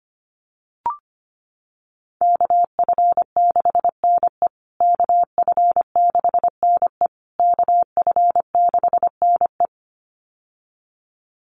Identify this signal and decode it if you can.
Morse code, heard as a keyed tone, about 25 wpm, 700 Hz. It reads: KF6NE KF6NE KF6NE
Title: USA Call Signs Repeated 3x 25wpm